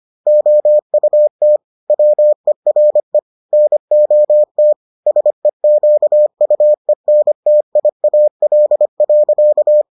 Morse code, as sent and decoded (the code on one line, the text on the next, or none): --- ..- - / .-- . .-. / -. --- - / ... --.- ..- . -. - .. .- .-.. .-.-.-
OUT WERE NOT SEQUENTIAL.